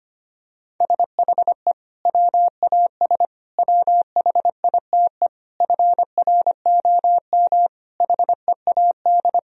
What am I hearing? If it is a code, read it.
Morse code, heard as a keyed tone, about 25 wpm, 700 Hz. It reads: SHE WAS WHITE FROM HEAD
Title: Sentences from Top 1000 Words - Review Part 3 - 25wpm